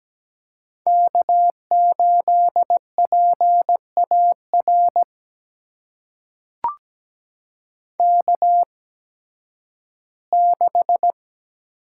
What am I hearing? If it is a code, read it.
Morse code, heard as a keyed tone, about 17 wpm, spaced as for 11 wpm, 700 Hz. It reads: K8PAR K6
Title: USA Call Signs - 5 Characters - 8x Character Spacing - 17wpm